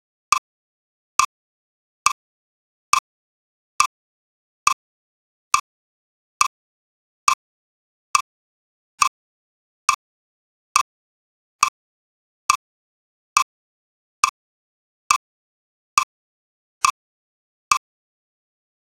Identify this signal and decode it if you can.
There is no code.